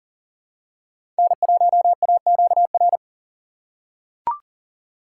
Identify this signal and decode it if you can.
Morse code, heard as a keyed tone, about 40 wpm, 700 Hz. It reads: N1AQR
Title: USA Call Signs - 5 Characters - 3x Character Spacing - 40wpm